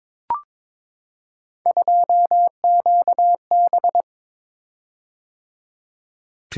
Morse code, heard as a keyed tone, about 22 wpm, 700 Hz. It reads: 2QB
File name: practice / 22wpm / Three Letter-Number Combinations - 22wpm